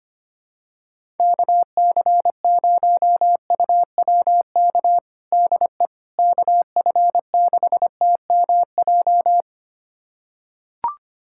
Morse code, heard as a keyed tone, about 25 wpm, 700 Hz. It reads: KC0UWK DE KF6TMJ
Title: USA Callsign DE Callsign - 6 Characters - 25wpm